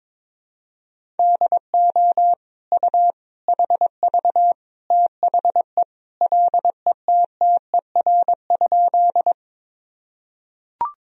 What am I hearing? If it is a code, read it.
Morse code, heard as a keyed tone, about 22 wpm, 700 Hz. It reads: DO U HV THE LETTER?
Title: Sentences from Top 300 Words and Abbreviations 22wpm